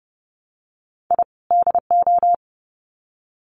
Morse code, heard as a keyed tone, about 30 wpm, 700 Hz. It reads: I DO
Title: Sets of 2 Words 30wpm